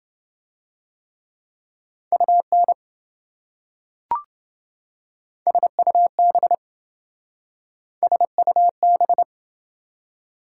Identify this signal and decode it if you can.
Morse code, heard as a keyed tone, about 28 wpm, 700 Hz. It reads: UN SUB SUB